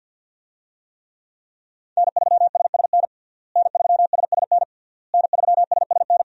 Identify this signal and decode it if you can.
Morse code, heard as a keyed tone, about 53 wpm, 700 Hz. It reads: N3SSN N3SSN N3SSN
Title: USA Call Signs - 5 Characters - Repeated 3x - 50wpm